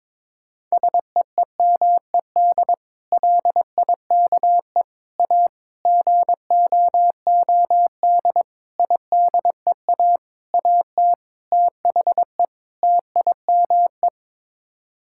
Morse code, heard as a keyed tone, about 22 wpm, 700 Hz. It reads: SEEMED LIKE A GOOD IDEA AT THE TIME
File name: practice / 22wpm / Sets of 8 Words - 22wpm